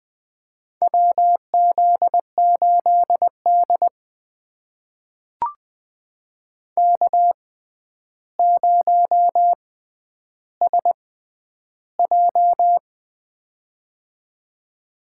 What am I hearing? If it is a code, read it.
Morse code, heard as a keyed tone, about 20 wpm, spaced as for 10 wpm, 700 Hz. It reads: WZ8D K0SJ
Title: USA Call Signs - 4 Characters - 6x Character Spacing - 20wpm